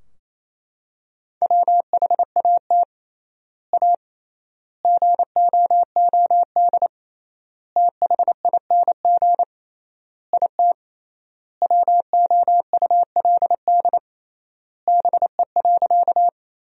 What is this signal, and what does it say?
Morse code, heard as a keyed tone, about 28 wpm, 700 Hz. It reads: WHAT A GOOD THING IT WOULD BE.
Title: 3x Word-Spacing - Sentences from Top 200 Words - 28wpm